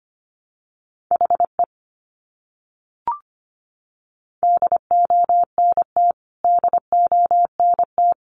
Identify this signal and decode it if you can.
Morse code, heard as a keyed tone, about 25 wpm, 700 Hz. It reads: HE DONT DONT
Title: Top 500 Words Repeated 25wpm